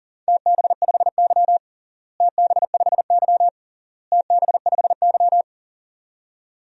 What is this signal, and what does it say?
Morse code, heard as a keyed tone, about 40 wpm, 700 Hz. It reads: TB5Y TB5Y TB5Y
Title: DXCC-100 Call Signs - 4 Characters - Repeated 3x - 40wpm